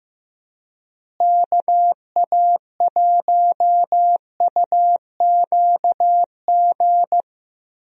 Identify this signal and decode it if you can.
Morse code, heard as a keyed tone, about 15 wpm, 700 Hz. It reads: KA1UQG
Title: USA Call Signs - 6 Characters - Repeated 3x - 15wpm